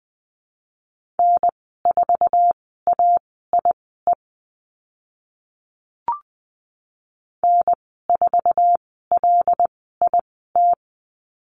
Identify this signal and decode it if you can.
Morse code, heard as a keyed tone, about 20 wpm, spaced as for 14 wpm, 700 Hz. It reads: N4AIE N4LIT
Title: 076 Lesson 13g - Letter L - Call Signs 20wpm